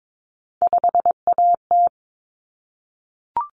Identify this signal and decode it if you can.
Morse code, heard as a keyed tone, about 22 wpm, 700 Hz. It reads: 5AT
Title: Common Cut Numbers - 3-Digit 22wpm